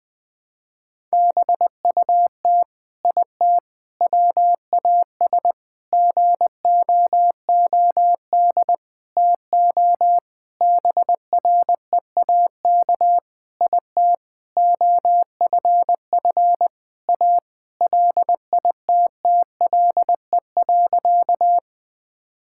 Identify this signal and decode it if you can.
Morse code, heard as a keyed tone, about 21 wpm, 700 Hz. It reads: BUT IT WAS GOOD TO BREAK IT OFF A LITTLE.
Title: Sentences from Top 1000 Words - Review Part 6 - 20wpm